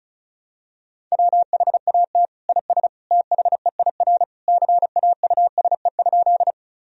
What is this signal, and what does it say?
Morse code, heard as a keyed tone, about 36 wpm, 700 Hz. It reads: WHAT IS THEIR CAUSE?
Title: Sentences from Top 1000 Words - Review Part 1 - 35wpm